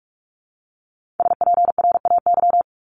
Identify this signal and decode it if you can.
Morse code, heard as a keyed tone, about 46 wpm, 700 Hz. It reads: SPRAY
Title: Multi-letter Phonemes 45wpm